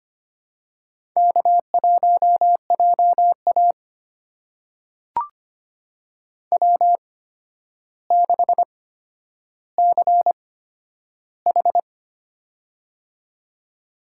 Morse code, heard as a keyed tone, about 25 wpm, spaced as for 11 wpm, 700 Hz. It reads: K1JA W6CH
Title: USA Call Signs - 4 Characters - 8x Character Spacing - 25wpm